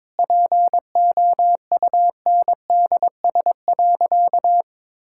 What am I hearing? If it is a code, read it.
Morse code, heard as a keyed tone, about 22 wpm, 700 Hz. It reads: POUNDS.
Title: Sentences from Top 1000 Words - Review Part 6 - 22wpm